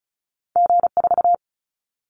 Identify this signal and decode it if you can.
Morse code, heard as a keyed tone, about 34 wpm, 700 Hz. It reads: G4
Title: Callsign Prefixes for Top 25 DXCC - 35wpm